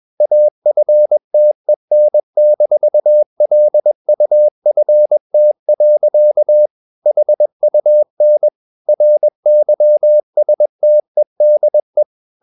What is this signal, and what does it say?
Morse code, heard as a keyed tone, about 21 wpm, 600 Hz. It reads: AFTEN-LUFT. HUN RYSTEDE
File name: Ferie QRS 21wpm